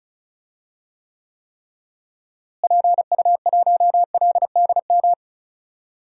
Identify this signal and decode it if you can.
Morse code, heard as a keyed tone, about 34 wpm, 700 Hz. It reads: PU1LDM
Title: DXCC-100 Call Signs - 6 Characters - Repeated 3x - 35wpm